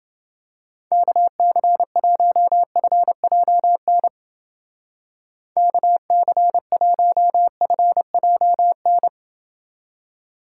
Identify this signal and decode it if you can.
Morse code, heard as a keyed tone, about 29 wpm, 700 Hz. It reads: KC1FJN KC1FJN